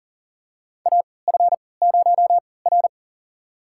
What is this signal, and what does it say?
Morse code, heard as a keyed tone, about 40 wpm, spaced as for 23 wpm, 700 Hz. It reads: AF0R
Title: USA Call Signs - 4 Characters - 3x Character Spacing - 40wpm